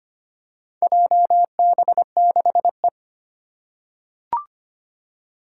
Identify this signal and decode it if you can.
Morse code, heard as a keyed tone, about 25 wpm, 700 Hz. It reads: JB6E
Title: Four Challenging Letter-Number Combinations 25wpm